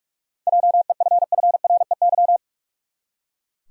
Morse code, heard as a keyed tone, about 41 wpm, 700 Hz. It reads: JEFFREY
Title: USA Common Names - 45wpm